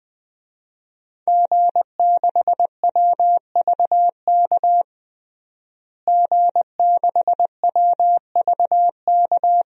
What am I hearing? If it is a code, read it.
Morse code, heard as a keyed tone, about 21 wpm, 700 Hz. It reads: G6WVK G6WVK
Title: DXCC-100 Call Signs - 5 Characters - Repeated 3x - 20wpm